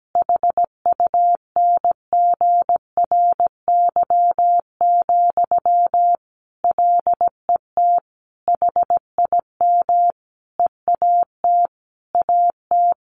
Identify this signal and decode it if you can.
Morse code, heard as a keyed tone, about 17 wpm, 700 Hz. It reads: HUNGRY, LET HIM EAT AT
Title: Sentences from Top 1400 Words 17wpm